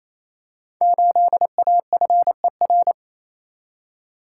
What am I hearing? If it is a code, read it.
Morse code, heard as a keyed tone, about 29 wpm, 700 Hz. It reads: 8AFER
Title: Code Groups - 5 Letter:Numbers - 28wpm